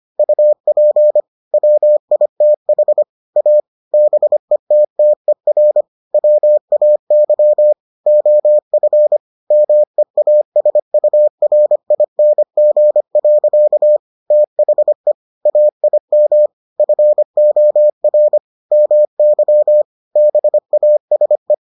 Morse code, heard as a keyed tone, about 25 wpm, 600 Hz. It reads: UP WITH A BETTER WAY OF MEASURING. THE AIM FOR MY BASE